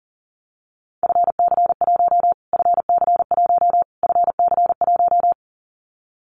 Morse code, heard as a keyed tone, about 40 wpm, 700 Hz. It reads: FC1 FC1 FC1
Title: Callsign Prefixes for Top 25 DXCC - Repeated 3x - 40wpm